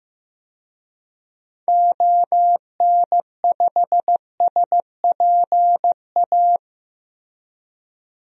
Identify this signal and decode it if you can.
Morse code, heard as a keyed tone, about 15 wpm, 700 Hz. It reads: ON5SPA
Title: DXCC-100 Call Signs - 6 Characters - Repeated 2x - 15wpm